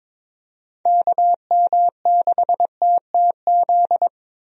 Morse code, heard as a keyed tone, about 22 wpm, 700 Hz. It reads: KM6TTZ
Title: USA Call Signs - 6 Characters - Repeated 2x - 22wpm